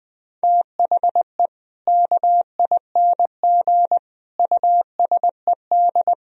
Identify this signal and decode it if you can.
Morse code, heard as a keyed tone, about 20 wpm, 700 Hz. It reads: THE KING USED
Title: Sentences from Top 1000 Words - Review Part 4 - 20wpm